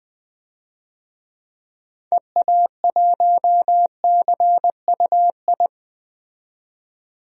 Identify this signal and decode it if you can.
Morse code, heard as a keyed tone, about 20 wpm, 700 Hz. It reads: EA1CUI
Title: DXCC-100 Call Signs - 6 Characters - 20wpm